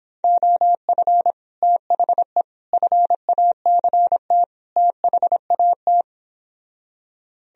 Morse code, heard as a keyed tone, about 26 wpm, 700 Hz. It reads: OF THE FACT THAT